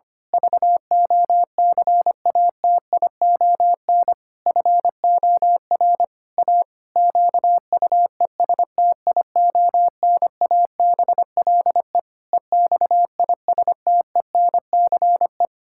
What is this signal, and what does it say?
Morse code, heard as a keyed tone, about 25 wpm, 700 Hz. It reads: VOCATION FOR A QUESTIONABLE EXISTENCE
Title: Pangram Challenge v2 - 25wpm